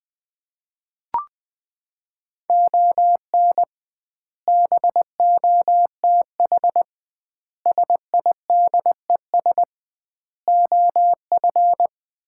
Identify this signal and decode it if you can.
Morse code, heard as a keyed tone, about 21 wpm, 700 Hz. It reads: ON BOTH SIDES OF
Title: 2x Word-Spacing - Sets of 5 Words - 20wpm